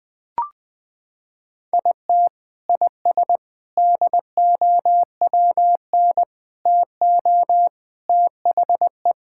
Morse code, heard as a keyed tone, about 20 wpm, 700 Hz. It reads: IT IS DOWN TO THE
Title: Sentences with 3-Word Phrases - 20wpm